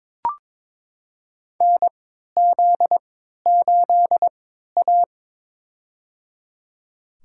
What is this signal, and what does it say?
Morse code, heard as a keyed tone, about 22 wpm, spaced as for 12 wpm, 700 Hz. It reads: NZ8A
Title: USA Call Signs - 4 Characters - 3x Character Spacing - 22wpm